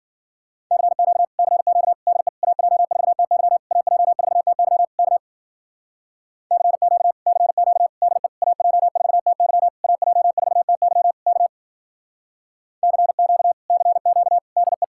Morse code, heard as a keyed tone, about 50 wpm, 700 Hz. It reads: CQ CQ DE AJ4TY AJ4TY K CQ CQ DE AJ4TY AJ4TY K CQ CQ DE
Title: Speed-Racing Common QSO Phrases Encore - 40wpm